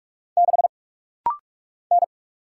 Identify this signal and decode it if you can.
Morse code, heard as a keyed tone, about 45 wpm, 700 Hz. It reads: 6 N
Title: Single Letter-Number Alt - Rapid-Fire 45wpm